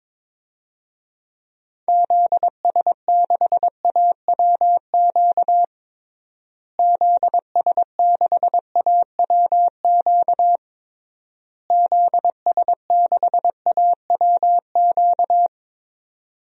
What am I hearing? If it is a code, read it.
Morse code, heard as a keyed tone, about 22 wpm, 700 Hz. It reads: ZS6AWQ ZS6AWQ ZS6AWQ
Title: DXCC-100 Call Signs - 6 Characters - Repeated 3x - 22wpm